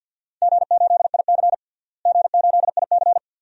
Morse code, heard as a keyed tone, about 52 wpm, 700 Hz. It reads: G8IC G8IC
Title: DXCC-100 Call Signs - 4 Characters - Repeated 2x - 50wpm